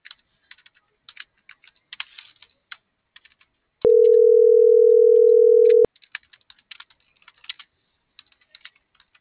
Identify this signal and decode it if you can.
Ringback tone (North America), 440 + 480 Hz.